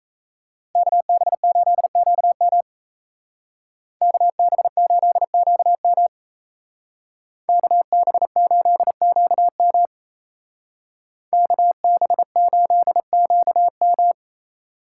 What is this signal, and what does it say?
Morse code, heard as a keyed tone, about 33 wpm, 700 Hz. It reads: KB8QM KB8QM KB8QM KB8QM
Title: Speed Racing - USA 2x2 Call Signs 28wpm